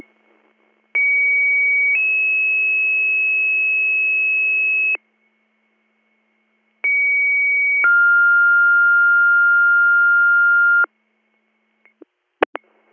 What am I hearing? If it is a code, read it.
Two-tone page, heard as a one-second tone, then a three-second tone, about 2250 then 2600 Hz, about 2250 then 1400 Hz. It is heard from 0.9 to 10.8 s.